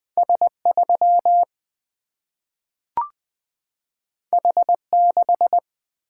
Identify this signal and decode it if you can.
Morse code, heard as a keyed tone, about 20 wpm, 700 Hz. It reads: S3 H6
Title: Single-Letter Single-Number 20wpm